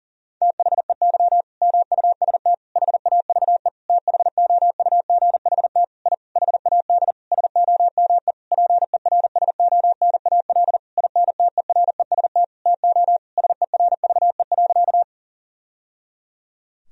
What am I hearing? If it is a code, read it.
Morse code, heard as a keyed tone, about 40 wpm, 700 Hz. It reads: THEY MUST HAVE THOUGHT I HAD SOME PERSONAL INTEREST TO SERVE.
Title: Sentences from Top 1000 Words - Review Part 4 - 40wpm